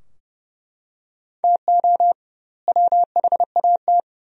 Morse code, heard as a keyed tone, about 31 wpm, 700 Hz. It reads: TO WHAT
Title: 2x Word-Spacing - Sentences from Top 200 Words - 30wpm